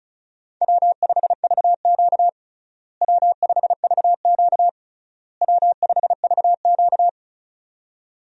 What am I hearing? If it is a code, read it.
Morse code, heard as a keyed tone, about 36 wpm, 700 Hz. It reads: W5VQ W5VQ W5VQ
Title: USA Call Signs - 4 Characters - Repeated 3x - 35wpm